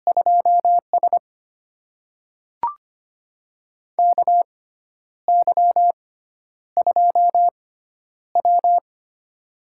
Morse code, heard as a keyed tone, about 25 wpm, spaced as for 10 wpm, 700 Hz. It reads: 2S KY2W
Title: USA Call Signs - 4 Characters - 6x Character Spacing - 25wpm